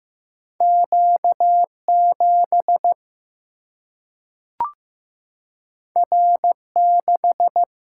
Morse code, heard as a keyed tone, about 15 wpm, 700 Hz. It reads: Q7 R6
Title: Single-Letter Single-Number 15wpm